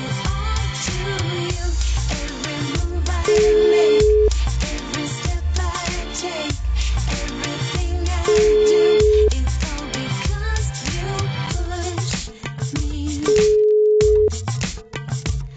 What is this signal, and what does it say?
Ringback tone (Europe, 425 Hz).